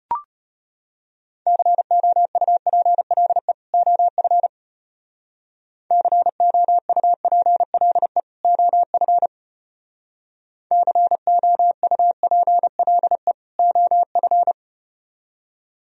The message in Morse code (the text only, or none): COUPLE OF COUPLE OF COUPLE OF